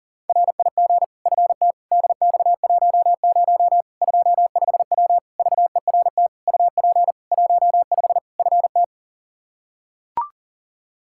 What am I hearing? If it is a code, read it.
Morse code, heard as a keyed tone, about 40 wpm, 700 Hz. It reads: RIG FT DX10 25W VERT UP 15 FT
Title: K1USN SST QSO Practice 40wpm